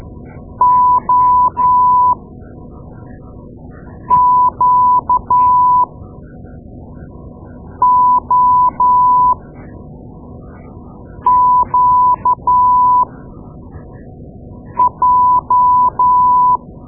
Morse code, heard as a keyed tone, about 12 wpm, spaced as for 4 wpm, 1000 Hz. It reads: OQOQJ
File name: Morse Lessons / Morse Lessons / Tape-2A